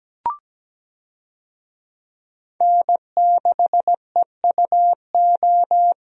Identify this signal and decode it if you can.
Morse code, heard as a keyed tone, about 17 wpm, 700 Hz. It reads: N6EUO
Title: USA Call Signs - 5 Characters - Repeated 3x - 17wpm